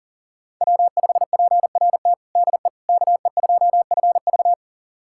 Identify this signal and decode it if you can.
Morse code, heard as a keyed tone, about 40 wpm, 700 Hz. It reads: W5PRT DE KE2FV
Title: USA Callsign DE Callsign - 5 Characters - 40wpm